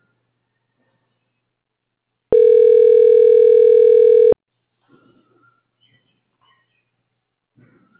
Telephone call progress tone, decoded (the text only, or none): ringback tone